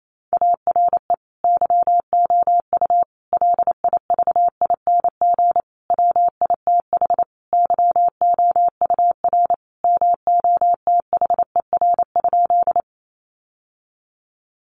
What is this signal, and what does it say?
Morse code, heard as a keyed tone, about 28 wpm, 700 Hz. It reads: ARE YOU LIVING WITH YOUR MOTHER?
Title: Sentences from Top 1100 Words 28wpm